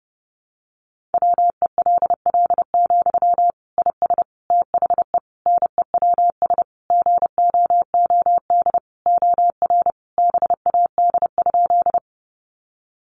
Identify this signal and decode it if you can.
Morse code, heard as a keyed tone, about 30 wpm, 700 Hz. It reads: WELL, IS THE NEWS GOOD OR BAD?
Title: Sentences from Top 800 Words 30wpm